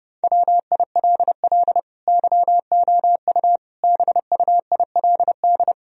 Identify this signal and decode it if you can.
Morse code, heard as a keyed tone, about 32 wpm, 700 Hz. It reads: WILL YOU BUILD
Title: Sentences from Top 1000 Words - Review Part 1 - 30wpm